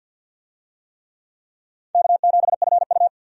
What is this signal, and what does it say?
Morse code, heard as a keyed tone, about 52 wpm, 700 Hz. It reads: K7FU
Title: USA Call Signs - 4 Characters - 50wpm